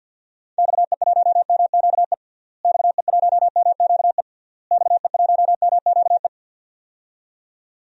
Morse code, heard as a keyed tone, about 50 wpm, 700 Hz. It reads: XE1MQE XE1MQE XE1MQE
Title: DXCC-100 Call Signs - 6 Characters - Repeated 3x - 50wpm